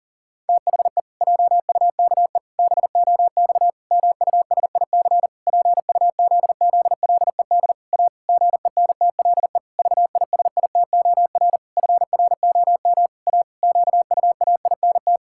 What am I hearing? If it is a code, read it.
Morse code, heard as a keyed tone, about 37 wpm, 700 Hz. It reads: THE JUKE BOX MUSIC PUZZLED A GENTLE VISITOR FROM A QUAINT
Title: Pangram Challenge v2 - 40wpm